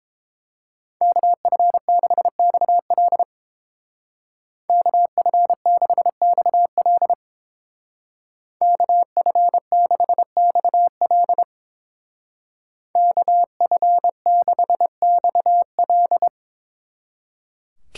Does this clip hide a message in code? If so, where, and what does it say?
Yes: Morse code from 1.0 to 16.3 s, heard as a keyed tone, about 31 wpm, 700 Hz. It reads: KF6XL KF6XL KF6XL KF6XL